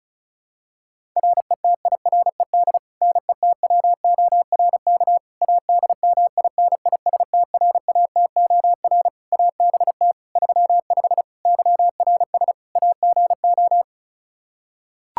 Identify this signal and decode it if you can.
Morse code, heard as a keyed tone, about 36 wpm, 700 Hz. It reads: RETIRED NETWORK ADMINISTRATOR ABT 35 YRS AGO
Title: Practice QSO - Exchange 3 - More Info - 35wpm